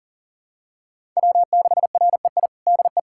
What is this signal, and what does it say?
Morse code, heard as a keyed tone, about 40 wpm, 700 Hz. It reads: W6REI DE
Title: USA Callsign DE Callsign 40wpm